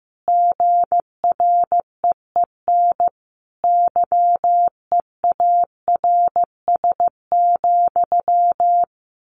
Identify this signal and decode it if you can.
Morse code, heard as a keyed tone, about 15 wpm, 700 Hz. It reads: GREEN YEARS,
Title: A Sound Of Thunder 15wpm